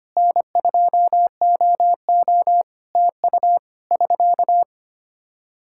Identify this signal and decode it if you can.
Morse code, heard as a keyed tone, about 25 wpm, 700 Hz. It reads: N2OO TU <SK>
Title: Practice QSO - Exchange 4 - Ending - 25wpm